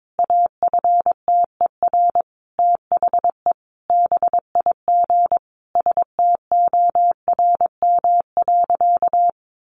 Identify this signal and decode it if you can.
Morse code, heard as a keyed tone, about 22 wpm, 700 Hz. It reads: AFTER THE BIG STORM.
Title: Sentences from Top 1500 Words 22wpm